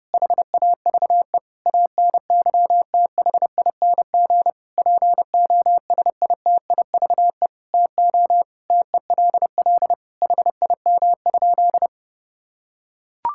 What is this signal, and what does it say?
Morse code, heard as a keyed tone, about 32 wpm, 700 Hz. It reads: HAVE ANYTHING POSITIVE TO TELL HIM?
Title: Sentences from Top 1000 Words - Review Part 5 - 30wpm